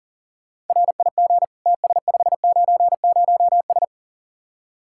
Morse code, heard as a keyed tone, about 50 wpm, 700 Hz. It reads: RIG TS590S
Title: Speed-Racing Common QSO Phrases Encore - 40wpm